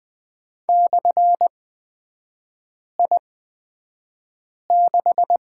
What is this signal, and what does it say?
Morse code, heard as a keyed tone, about 20 wpm, spaced as for 5 wpm, 700 Hz. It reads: /I6
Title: 243 Lesson 39e - Letter z - Three Characters 20wpm